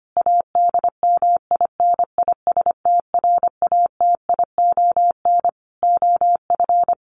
Morse code, heard as a keyed tone, about 25 wpm, 700 Hz. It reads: ADMINISTRATION OF
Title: Sentences from Top 1300 Words 25wpm